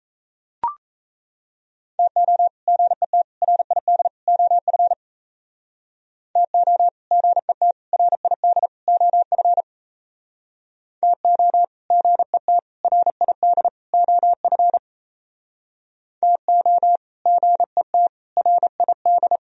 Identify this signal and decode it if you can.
Morse code, heard as a keyed tone, about 36 wpm, 700 Hz. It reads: TO GET RID OF TO GET RID OF TO GET RID OF TO GET RID